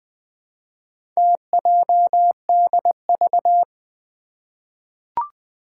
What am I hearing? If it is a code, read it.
Morse code, heard as a keyed tone, about 20 wpm, 700 Hz. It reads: TJDV
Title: Four Challenging Letter-Number Combinations 20wpm